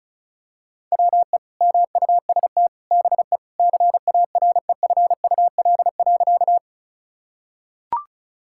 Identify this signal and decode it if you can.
Morse code, heard as a keyed tone, about 36 wpm, 700 Hz. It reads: WE MUST BE CAREFUL.
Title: Sentences from Top 1600 Words 35wpm